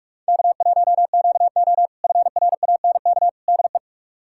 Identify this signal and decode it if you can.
Morse code, heard as a keyed tone, about 46 wpm, 700 Hz. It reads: K1QO FRANK DE
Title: K1USN SST QSO Practice - 45wpm